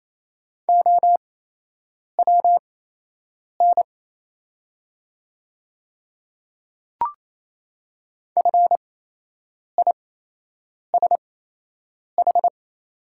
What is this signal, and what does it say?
Morse code, heard as a keyed tone, about 28 wpm, spaced as for 8 wpm, 700 Hz. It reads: OWN FISH
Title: 8x Character Spacing - Top 300 Words - 28wpm